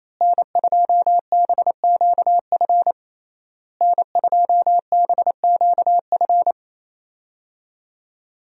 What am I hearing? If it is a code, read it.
Morse code, heard as a keyed tone, about 27 wpm, 700 Hz. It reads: N2BQF N2BQF
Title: USA Call Signs - 5 Characters - Repeated 3x - 28wpm